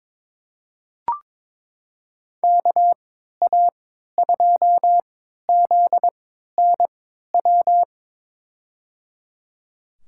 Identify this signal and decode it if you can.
Morse code, heard as a keyed tone, about 21 wpm, spaced as for 12 wpm, 700 Hz. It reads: KA2ZNW
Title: USA Call Signs - 6 Characters - 3x Character Spacing - 22wpm